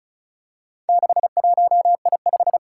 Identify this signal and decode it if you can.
Morse code, heard as a keyed tone, about 36 wpm, 700 Hz. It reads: 61I5
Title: Four Challenging Letter-Number Combinations 35wpm